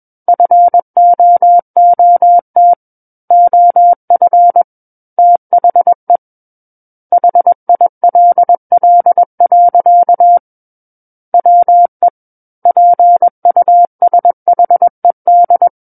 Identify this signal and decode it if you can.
Morse code, heard as a keyed tone, about 21 wpm, 700 Hz. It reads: FOOT OF THE HILL. WE PUSHED
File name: Book One / 12-Chapter 12 - 21 WPM